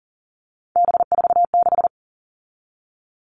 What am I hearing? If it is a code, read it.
Morse code, heard as a keyed tone, about 40 wpm, 700 Hz. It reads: B46
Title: All Cut Numbers - 3-Digit 40wpm